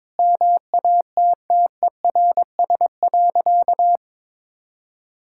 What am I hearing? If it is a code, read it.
Morse code, heard as a keyed tone, about 22 wpm, 700 Hz. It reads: MATTERS.
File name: practice / 22wpm / Sentences with 4-Word Phrases - 22wpm